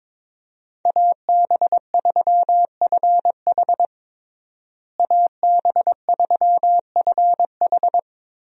Morse code, heard as a keyed tone, about 22 wpm, 700 Hz. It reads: AB3FH AB3FH
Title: USA Call Signs - 5 Characters - Repeated 2x - 22wpm